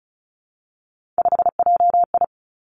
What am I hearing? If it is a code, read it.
Morse code, heard as a keyed tone, about 36 wpm, 700 Hz. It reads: 5JI
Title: Three Challenging Letter-Number Combinations 35wpm